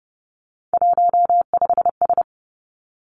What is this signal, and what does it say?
Morse code, heard as a keyed tone, about 30 wpm, 700 Hz. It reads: 15S
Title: Three Challenging Letter-Number Combinations 30wpm